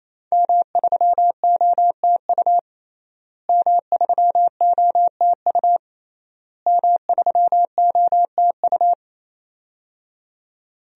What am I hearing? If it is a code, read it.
Morse code, heard as a keyed tone, about 28 wpm, 700 Hz. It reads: M3OTU M3OTU M3OTU
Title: DXCC-100 Call Signs - 5 Characters - Repeated 3x - 28wpm